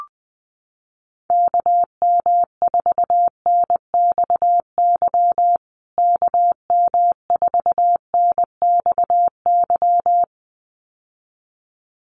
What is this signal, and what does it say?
Morse code, heard as a keyed tone, about 20 wpm, 700 Hz. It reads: KM4NXY KM4NXY
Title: USA Call Signs Repeated 2x 20wpm